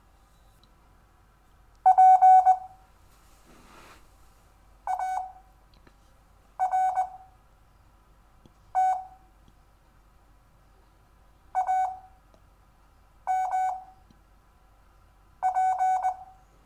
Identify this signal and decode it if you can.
Morse code, heard as a keyed tone, about 20 wpm, spaced as for 5 wpm, 750 Hz. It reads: P ART AMP
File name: cw / Lesson4 REVIEW